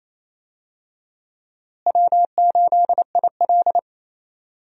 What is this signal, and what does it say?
Morse code, heard as a keyed tone, about 28 wpm, 700 Hz. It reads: W8IL
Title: USA Call Signs - 4 Characters - Repeated 2x - 28wpm